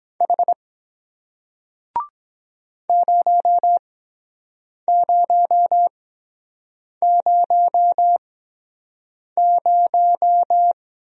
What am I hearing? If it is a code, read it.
Morse code, heard as a keyed tone, about 23 wpm, spaced as for 7 wpm, 700 Hz. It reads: H 0000